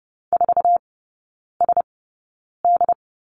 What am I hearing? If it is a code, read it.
Morse code, heard as a keyed tone, about 30 wpm, spaced as for 9 wpm, 700 Hz. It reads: 4SD
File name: Lesson 18e - Letter m - Three Characters 30wpm